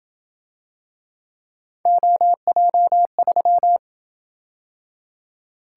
Morse code, heard as a keyed tone, about 28 wpm, 700 Hz. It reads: OJ3